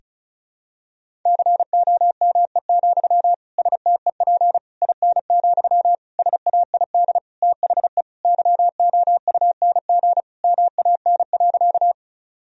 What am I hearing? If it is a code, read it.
Morse code, heard as a keyed tone, about 34 wpm, 700 Hz. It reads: COME, STEP IN, SAID THE YOUNG MAN.
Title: Sentences from Top 1000 Words - Review Part 2 - 35wpm